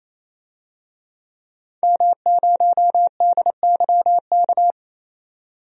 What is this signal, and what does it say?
Morse code, heard as a keyed tone, about 28 wpm, 700 Hz. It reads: M0DYK
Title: DXCC-100 Call Signs - 5 Characters - 28wpm